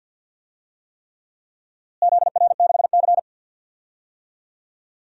Morse code, heard as a keyed tone, about 53 wpm, 700 Hz. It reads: ZR6C
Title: DXCC-100 Call Signs - 4 Characters - Repeated 3x - 50wpm